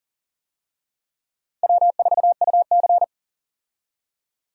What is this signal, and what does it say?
Morse code, heard as a keyed tone, about 40 wpm, 700 Hz. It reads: W4UC
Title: USA Call Signs - 4 Characters - 40wpm